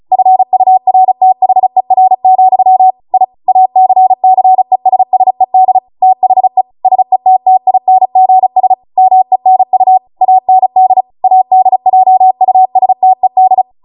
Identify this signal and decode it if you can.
Morse code, heard as a keyed tone, about 36 wpm, 750 Hz. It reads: FURTHER, I ACCESSED THE SETTINGS MENU AND ADJUSTED